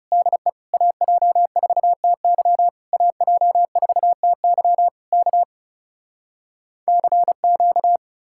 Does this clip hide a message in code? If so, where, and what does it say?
Yes: Morse code from 0.1 to 8.0 s, heard as a keyed tone, about 40 wpm, 700 Hz. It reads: DE AJ4TY AJ4TY K CQ